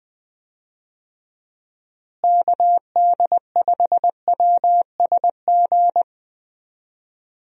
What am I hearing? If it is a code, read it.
Morse code, heard as a keyed tone, about 21 wpm, 700 Hz. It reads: KD5WSG
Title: USA Call Signs - 6 Characters - Repeated 3x - 20wpm